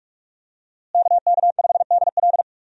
Morse code, heard as a keyed tone, about 46 wpm, 700 Hz. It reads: KK5DL
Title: USA Call Signs - 5 Characters - Repeated 3x - 45wpm